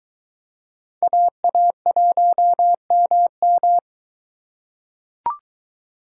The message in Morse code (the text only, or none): AA1MM